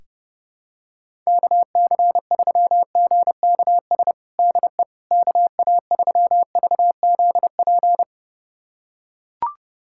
Morse code, heard as a keyed tone, about 32 wpm, 700 Hz. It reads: KC3GKS DE KA3VZP
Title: USA Callsign DE Callsign - 6 Characters - 30wpm